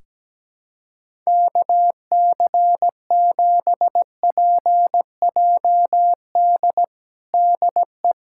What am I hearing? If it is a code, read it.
Morse code, heard as a keyed tone, about 17 wpm, 700 Hz. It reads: KC7PJD DE
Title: USA Callsign DE Callsign 17wpm